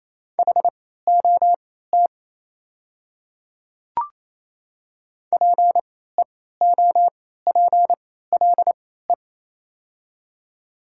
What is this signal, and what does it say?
Morse code, heard as a keyed tone, about 28 wpm, spaced as for 16 wpm, 700 Hz. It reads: HOT PEOPLE
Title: 3x Character Spacing - Top 300 Words - 28wpm